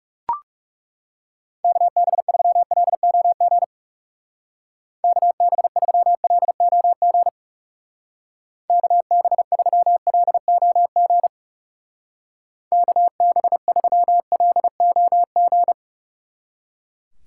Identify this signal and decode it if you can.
Morse code, heard as a keyed tone, about 36 wpm, 700 Hz. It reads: KB3LOG KB3LOG KB3LOG KB3LOG
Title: Speed Racing - USA 2x3 Call Signs 30wpm